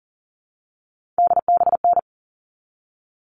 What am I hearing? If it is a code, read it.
Morse code, heard as a keyed tone, about 40 wpm, 700 Hz. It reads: DBN
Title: All Cut Numbers - 3-Digit 40wpm